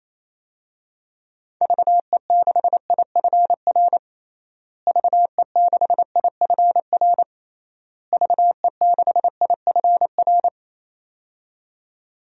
Morse code, heard as a keyed tone, about 28 wpm, 700 Hz. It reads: VE6IFR VE6IFR VE6IFR
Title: DXCC-100 Call Signs - 6 Characters - Repeated 3x - 28wpm